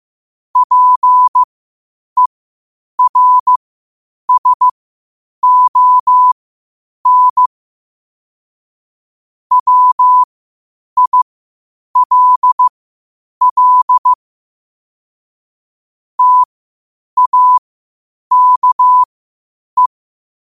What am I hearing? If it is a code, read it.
Morse code, heard as a keyed tone, about 15 wpm, spaced as for 8 wpm, 1000 Hz. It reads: PERSON WILL TAKE
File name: qotdfiles / 8wpm